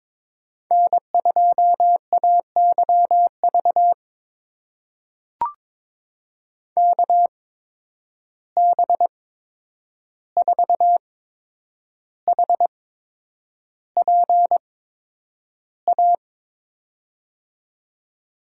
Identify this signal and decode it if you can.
Morse code, heard as a keyed tone, about 22 wpm, spaced as for 9 wpm, 700 Hz. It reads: N2AYV KB4HPA
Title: USA Call Signs - 8x Character Spacing - 22wpm